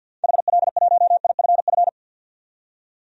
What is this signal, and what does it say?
Morse code, heard as a keyed tone, about 49 wpm, 700 Hz. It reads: HL1IFF
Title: DXCC-100 Call Signs - 6 Characters - 50wpm